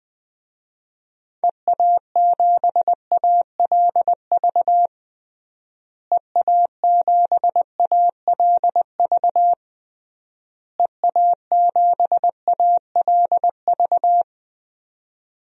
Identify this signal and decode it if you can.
Morse code, heard as a keyed tone, about 21 wpm, 700 Hz. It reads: EA7ALV EA7ALV EA7ALV
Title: DXCC-100 Call Signs - 6 Characters - Repeated 3x - 20wpm